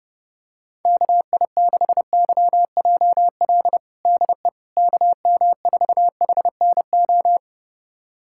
Morse code, heard as a keyed tone, about 30 wpm, 700 Hz. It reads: KI6YJL DE KM4HNO
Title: USA Callsign DE Callsign - 6 Characters - 30wpm